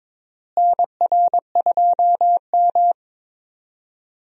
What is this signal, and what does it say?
Morse code, heard as a keyed tone, about 22 wpm, 700 Hz. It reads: NR2M
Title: USA Call Signs - 4 Characters - Repeated 3x - 22wpm